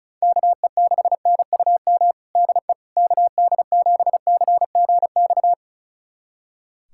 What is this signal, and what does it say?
Morse code, heard as a keyed tone, about 36 wpm, 700 Hz. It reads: KE6NUM DE KD7CGX
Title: USA Callsign DE Callsign - 6 Characters - 35wpm